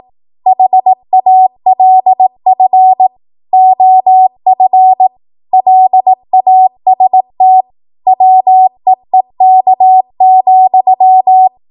Morse code, heard as a keyed tone, about 18 wpm, 750 Hz. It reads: HALF OF LAST WEEK,
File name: Archive / 18 WPM / 18WPM